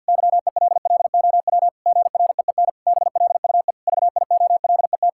Morse code, heard as a keyed tone, about 52 wpm, 700 Hz. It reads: YELLOW GREEN BLUE VIOLET
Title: Pangram Challenge v2 - 50wpm